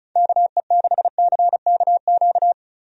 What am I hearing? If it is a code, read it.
Morse code, heard as a keyed tone, about 36 wpm, 700 Hz. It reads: KE6CKQ
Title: USA Call Signs - 6 Characters - 35wpm